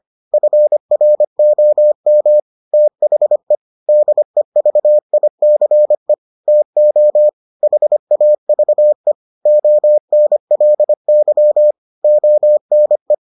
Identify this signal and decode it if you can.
Morse code, heard as a keyed tone, about 25 wpm, 600 Hz. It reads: FROM THE DEVICE TO HAVE ONLY ONE